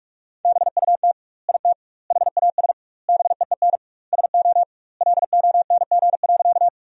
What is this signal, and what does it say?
Morse code, heard as a keyed tone, about 46 wpm, 700 Hz. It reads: BUT IT HAS BEEN SO LONG.
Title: 2x Word-Spacing - Sentences from Top 100 Words - 45wpm